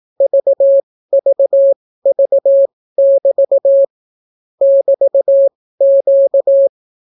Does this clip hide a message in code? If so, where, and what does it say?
Yes: Morse code from 0.2 to 6.7 s, heard as a keyed tone, about 18 wpm, spaced as for 15 wpm, 550 Hz. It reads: VVV<BT> <BT>Q